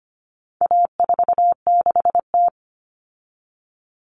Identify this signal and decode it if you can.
Morse code, heard as a keyed tone, about 25 wpm, 700 Hz. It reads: A46T
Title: Common Cut Numbers - 4-Digit 25wpm